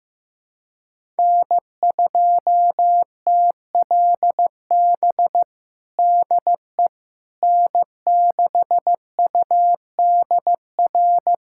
Morse code, heard as a keyed tone, about 15 wpm, 700 Hz. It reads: N2TLB DE N6UDR
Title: USA Callsign DE Callsign - 5 Characters - 15wpm